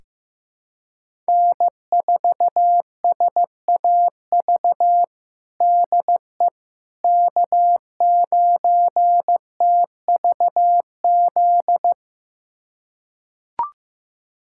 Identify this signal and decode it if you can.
Morse code, heard as a keyed tone, about 15 wpm, 700 Hz. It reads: N4SAV DE K9TVZ
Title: USA Callsign DE Callsign - 5 Characters - 15wpm